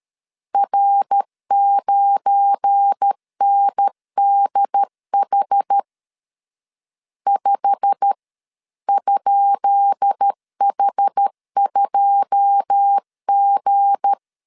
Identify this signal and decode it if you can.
Morse code, heard as a keyed tone, about 13 wpm, 800 Hz. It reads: R9NDH 5?H2G
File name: LEZIONI / 10 SIGNS / 10D COMA 12wpm 2544